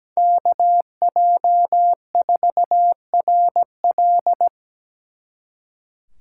Morse code, heard as a keyed tone, about 20 wpm, 700 Hz. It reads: KJ4RL
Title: Speed Racing - USA 2x2 Call Signs 17wpm